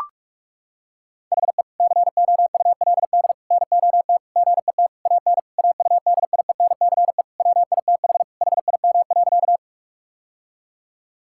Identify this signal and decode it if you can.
Morse code, heard as a keyed tone, about 41 wpm, 700 Hz. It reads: HE COULD NOT GET AN AUDIENCE WITH HIM.
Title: Sentences from Top 1000 Words - Review Part 5 - 45wpm